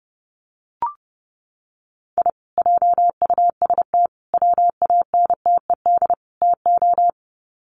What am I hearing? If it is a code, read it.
Morse code, heard as a keyed tone, about 29 wpm, 700 Hz. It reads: I JUST WANTED TO
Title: Sets of 4 Words 30wpm